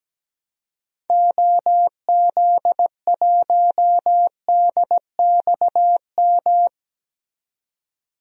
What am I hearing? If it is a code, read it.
Morse code, heard as a keyed tone, about 17 wpm, 700 Hz. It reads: OZ1DXM
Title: DXCC-100 Call Signs - 6 Characters - 17wpm